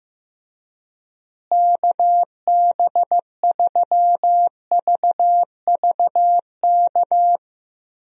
Morse code, heard as a keyed tone, about 15 wpm, 700 Hz. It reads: KB3VVK
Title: USA Call Signs - 6 Characters - Repeated 3x - 15wpm